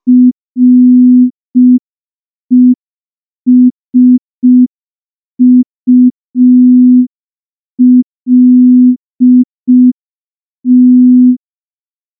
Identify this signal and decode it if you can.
Morse code, heard as a keyed tone, about 5 wpm, 250 Hz. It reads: RESULT
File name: morse / morse-training-05WPM